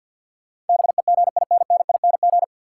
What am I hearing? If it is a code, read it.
Morse code, heard as a keyed tone, about 48 wpm, 700 Hz. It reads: BEGINNING
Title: Sentences from Top 800 Words - New Word Drill - 50wpm